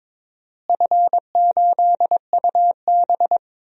Morse code, heard as a keyed tone, about 22 wpm, 700 Hz. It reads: F8UB
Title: DXCC-100 Call Signs - 4 Characters - Repeated 2x - 22wpm